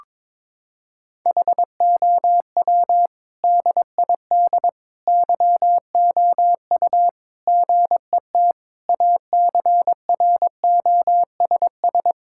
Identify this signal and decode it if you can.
Morse code, heard as a keyed tone, about 21 wpm, 700 Hz. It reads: HOW DID YOU GET ACROSS
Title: Sentences from Top 1000 Words - Review Part 5 - 22wpm